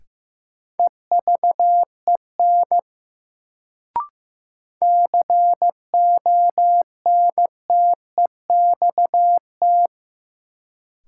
Morse code, heard as a keyed tone, about 16 wpm, 700 Hz. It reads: EVEN CONTEXT
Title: Sets of 4 Words - New Word Drill - 15wpm